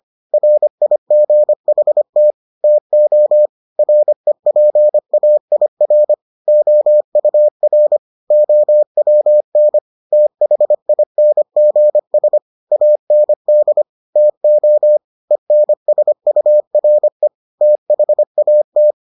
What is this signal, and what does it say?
Morse code, heard as a keyed tone, about 25 wpm, 600 Hz. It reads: RIGHT TO REPAIR OUR OWN THINGS AND TO ENSURE THAT